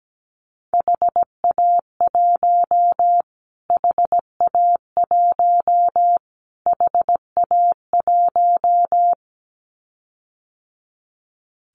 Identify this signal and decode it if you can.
Morse code, heard as a keyed tone, about 17 wpm, 700 Hz. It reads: HA1 HA1 HA1
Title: Callsign Prefixes for Top 25 DXCC - Repeated 3x - 17wpm